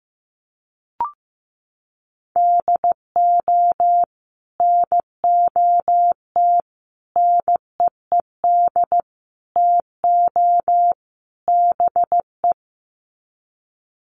Morse code, heard as a keyed tone, about 15 wpm, 700 Hz. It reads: DO NOT NEED TO BE
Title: Sets of 5 Words Encore 15wpm